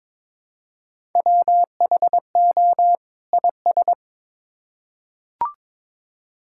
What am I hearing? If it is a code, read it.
Morse code, heard as a keyed tone, about 22 wpm, 700 Hz. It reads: WHO IS
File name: Build A Phrase - 2-Word Phrases - 22wpm